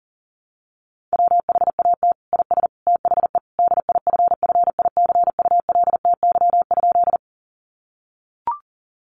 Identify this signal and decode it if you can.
Morse code, heard as a keyed tone, about 40 wpm, 700 Hz. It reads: WHAT IS THE DIFFICULTY?
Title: Sentences from Top 1500 Words 40wpm